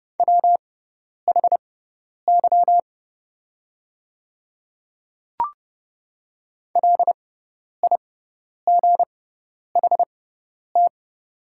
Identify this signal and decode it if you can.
Morse code, heard as a keyed tone, about 30 wpm, spaced as for 10 wpm, 690 Hz. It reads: WHY LIGHT
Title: 6x Character Spacing - Top 200 Words - 30wpm